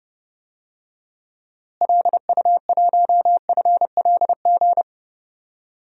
Morse code, heard as a keyed tone, about 30 wpm, 700 Hz. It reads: LU1FLG
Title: DXCC-100 Call Signs - 6 Characters - Repeated 3x - 30wpm